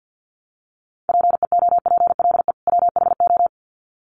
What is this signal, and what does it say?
Morse code, heard as a keyed tone, about 50 wpm, 700 Hz. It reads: PEOPLE WHO
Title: Sets of 2 Words 50wpm